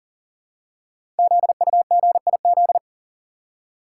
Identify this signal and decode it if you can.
Morse code, heard as a keyed tone, about 42 wpm, 700 Hz. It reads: ZUGIZ
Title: Code Groups - 5 Letter:Numbers - 40wpm